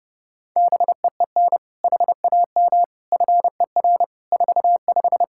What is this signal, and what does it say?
Morse code, heard as a keyed tone, about 30 wpm, 700 Hz. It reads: BEEN HAM FER 45
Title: Practice QSO - Exchange 3 - More Info - 30wpm